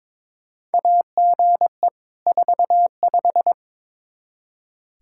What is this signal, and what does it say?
Morse code, heard as a keyed tone, about 22 wpm, 700 Hz. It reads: AGE 45
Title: Practice QSO - Exchange 3 - More Info - 22wpm